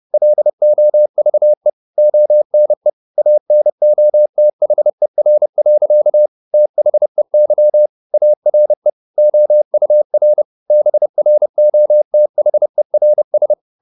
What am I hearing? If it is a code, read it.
Morse code, heard as a keyed tone, about 30 wpm, 600 Hz. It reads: LOVE ONE ANOTHER. THEY ARE OUR BROTHERS
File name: the old man and the sea / the old man and the sea 1 30wpm